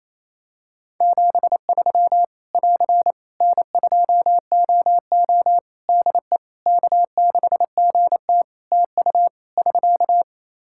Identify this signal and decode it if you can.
Morse code, heard as a keyed tone, about 28 wpm, 700 Hz. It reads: 73 <AR> N2OO DE K6GT TU <SK>
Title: Practice QSO - Exchange 4 - Ending - 28wpm